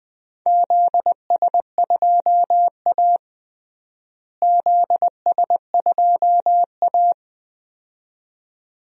Morse code, heard as a keyed tone, about 19 wpm, 700 Hz. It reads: ZS2A ZS2A
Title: DXCC-100 Call Signs - 4 Characters - Repeated 2x - 20wpm